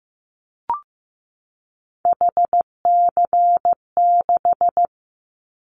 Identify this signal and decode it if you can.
Morse code, heard as a keyed tone, about 15 wpm, 700 Hz. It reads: HC6
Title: Callsign Prefixes for Top 100 DXCC - 15wpm